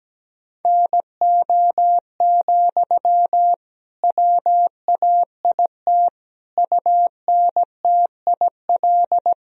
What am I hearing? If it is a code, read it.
Morse code, heard as a keyed tone, about 17 wpm, 700 Hz. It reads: NO, WAIT UNTIL
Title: Sentences from Top 1000 Words - Review Part 1 - 17wpm